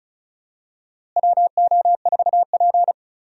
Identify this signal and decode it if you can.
Morse code, heard as a keyed tone, about 34 wpm, 700 Hz. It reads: WO4P
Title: USA Call Signs - 4 Characters - Repeated 3x - 35wpm